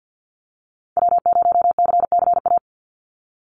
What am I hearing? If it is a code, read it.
Morse code, heard as a keyed tone, about 48 wpm, 700 Hz. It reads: W0CCA
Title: NA SOTA Activator Callsigns 50wpm